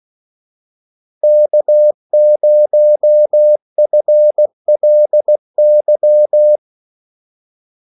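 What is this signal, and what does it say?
Morse code, heard as a keyed tone, about 16 wpm, 600 Hz. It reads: K0FLY
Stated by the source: Morse call sign practice at 16 wpm.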